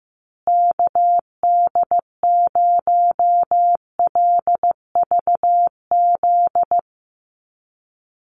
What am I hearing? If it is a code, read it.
Morse code, heard as a keyed tone, about 15 wpm, 700 Hz. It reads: KD0LVZ
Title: USA Call Signs Repeated 3x 15wpm